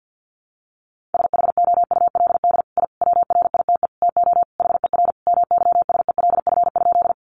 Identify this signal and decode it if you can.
Morse code, heard as a keyed tone, about 50 wpm, 700 Hz. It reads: SHOULD I WRITE TO HER MYSELF?
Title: Sentences from Top 1300 Words 50wpm